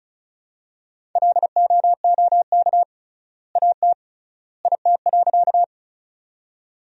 Morse code, heard as a keyed tone, about 36 wpm, 700 Hz. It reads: LOOK AT IT.
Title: 3x Word-Spacing - Sentences from Top 100 Words - 35wpm